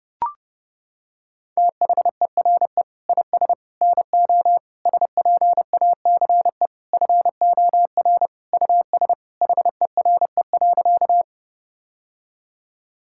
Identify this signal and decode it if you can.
Morse code, heard as a keyed tone, about 30 wpm, 700 Hz. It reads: THERE IS NO SPACE FOR US HERE.
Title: Sentences from Top 1000 Words - Review Part 5 - 30wpm